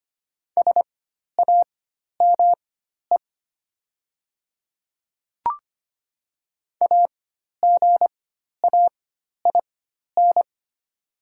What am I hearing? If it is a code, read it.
Morse code, heard as a keyed tone, about 25 wpm, spaced as for 12 wpm, 700 Hz. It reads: SAME AGAIN
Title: 4x Character Spacing - Top 200 Words - 25wpm